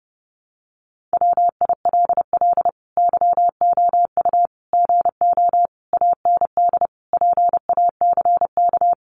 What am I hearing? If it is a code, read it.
Morse code, heard as a keyed tone, about 30 wpm, 700 Hz. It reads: WILL YOU GO AND PACK
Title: Sentences from Top 1300 Words 30wpm